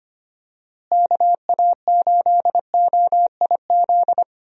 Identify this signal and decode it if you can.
Morse code, heard as a keyed tone, about 25 wpm, 700 Hz. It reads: KA8OIZ
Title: USA Call Signs - 6 Characters - Repeated 3x - 25wpm